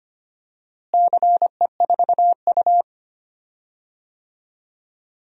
Morse code, heard as a keyed tone, about 25 wpm, 700 Hz. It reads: CE4U
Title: DXCC-100 Call Signs - 4 Characters - Repeated 2x - 25wpm